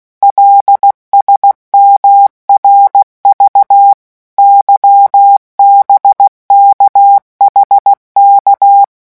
Morse code, heard as a keyed tone, about 16 wpm, 800 Hz. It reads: LSMRV YBKHK
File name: morse mp3 / revision5-vitesse 15 mots